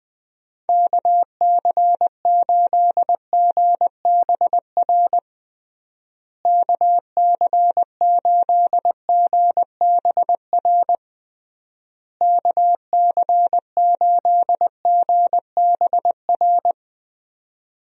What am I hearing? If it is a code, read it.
Morse code, heard as a keyed tone, about 19 wpm, 700 Hz. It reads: KC8GBR KC8GBR KC8GBR
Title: USA Call Signs - 6 Characters - Repeated 3x - 20wpm